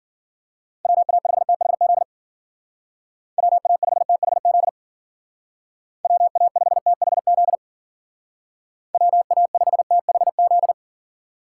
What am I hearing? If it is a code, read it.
Morse code, heard as a keyed tone, about 50 wpm, 700 Hz. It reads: WA5THZ WA5THZ WA5THZ WA5THZ
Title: Speed Racing - USA 2x3 Call Signs 40wpm